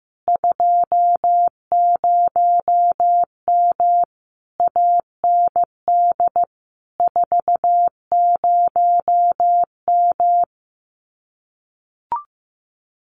Morse code, heard as a keyed tone, about 15 wpm, 700 Hz. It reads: 20M AND 40M
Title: Common QSO Phrases Encore 15wpm